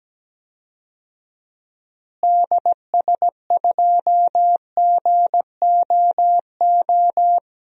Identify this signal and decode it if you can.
Morse code, heard as a keyed tone, about 17 wpm, 700 Hz. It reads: DS2GOO
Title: DXCC-100 Call Signs - 6 Characters - Repeated 3x - 17wpm